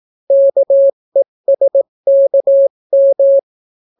Morse code, heard as a keyed tone, about 18 wpm, 550 Hz. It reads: KESKM